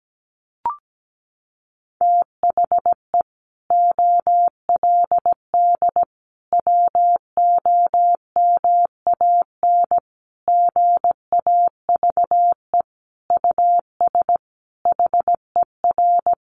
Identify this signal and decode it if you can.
Morse code, heard as a keyed tone, about 17 wpm, 700 Hz. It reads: THE OLD WOMAN GAVE US HER
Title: Sentences from Top 1100 Words 17wpm